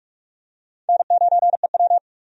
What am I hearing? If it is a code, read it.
Morse code, heard as a keyed tone, about 46 wpm, 700 Hz. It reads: N9EW
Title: USA Call Signs - 4 Characters - 45wpm